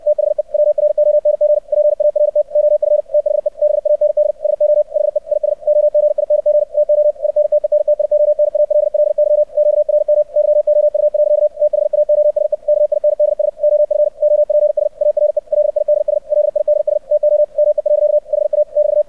Morse code, heard as a keyed tone, about 64 wpm, 600 Hz. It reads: THE WRONG PART OF THE BAND SO HE IS OGING TO INTENTIONALLY QRM YOU. THAOSE KINDS OF OPS ARE LIDS LIDS TO ME. HA <BT>
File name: archived / CRSnet